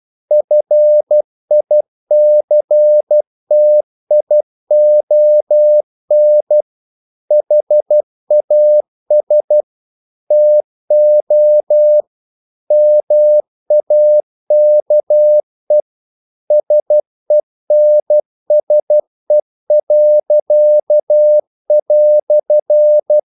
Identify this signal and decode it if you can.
Morse code, heard as a keyed tone, about 12 wpm, 600 Hz. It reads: FICTION HAS TO MAKE SENSE."
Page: English quotations in CW (telegraphy - morse)